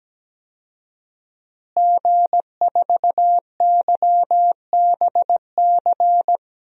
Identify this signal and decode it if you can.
Morse code, heard as a keyed tone, about 17 wpm, 700 Hz. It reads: G4YBC
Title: DXCC-100 Call Signs - 5 Characters - Repeated 2x - 17wpm